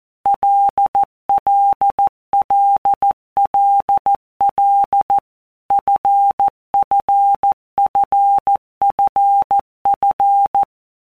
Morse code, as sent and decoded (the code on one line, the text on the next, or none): .-.. .-.. .-.. .-.. .-.. / ..-. ..-. ..-. ..-. ..-.
LLLLL FFFFF